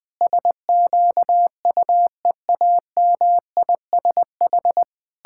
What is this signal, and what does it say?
Morse code, heard as a keyed tone, about 20 wpm, 700 Hz. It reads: SQUEAMISH
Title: Pangram Challenge v2 - 20wpm